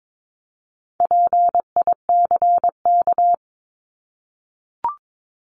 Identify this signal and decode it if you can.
Morse code, heard as a keyed tone, about 22 wpm, 700 Hz. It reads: PICK
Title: Top 751-1000 Words 22wpm